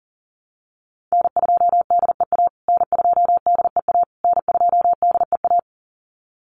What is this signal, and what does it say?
Morse code, heard as a keyed tone, about 40 wpm, 700 Hz. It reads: N2DEA N2DEA N2DEA
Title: USA Call Signs Repeated 3x 40wpm